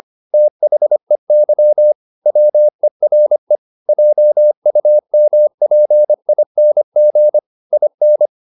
Morse code, heard as a keyed tone, about 25 wpm, 600 Hz. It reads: THEY WERE JUMPING IN
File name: the old man and the sea part 1 25wpm